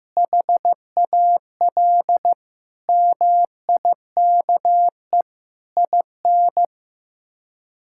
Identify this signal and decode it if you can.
Morse code, heard as a keyed tone, about 15 wpm, 700 Hz. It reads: HAL MIKE IN
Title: K1USN SST QSO Practice - 15wpm